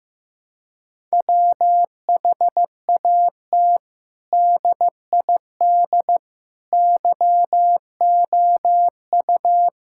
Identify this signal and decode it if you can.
Morse code, heard as a keyed tone, about 15 wpm, 700 Hz. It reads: WHAT DID YOU
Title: Sentences from Top 1000 Words - Review Part 6 - 15wpm